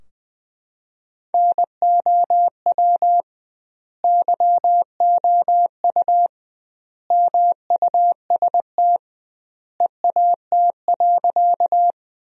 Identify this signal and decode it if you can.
Morse code, heard as a keyed tone, about 20 wpm, 700 Hz. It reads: NOW YOU MUST EAT.
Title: 2x Word-Spacing - Sentences from Top 300 Words - 20wpm